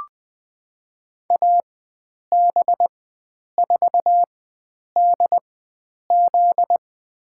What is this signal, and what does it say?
Morse code, heard as a keyed tone, about 21 wpm, spaced as for 9 wpm, 700 Hz. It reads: AB4DZ
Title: USA Call Signs - 4x Character Spacing - 20wpm